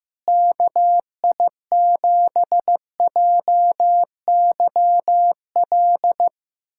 Morse code, heard as a keyed tone, about 15 wpm, 700 Hz. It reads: KI7JYL
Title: USA Callsign DE Callsign 15wpm